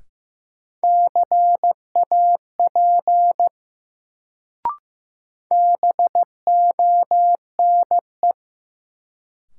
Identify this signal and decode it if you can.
Morse code, heard as a keyed tone, about 15 wpm, 700 Hz. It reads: CAP BONE
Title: Sentences from Top 1600 Words - New Word Drill - 15wpm